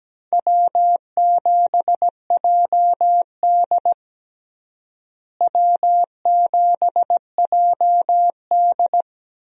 Morse code, heard as a keyed tone, about 17 wpm, 700 Hz. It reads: W7JD W7JD
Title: USA Call Signs - 4 Characters - Repeated 3x - 17wpm